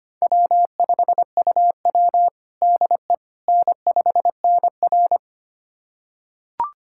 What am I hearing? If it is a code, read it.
Morse code, heard as a keyed tone, about 25 wpm, 700 Hz. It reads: W5UW DE N5NR
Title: USA Callsign DE Callsign - 4 Characters - 25wpm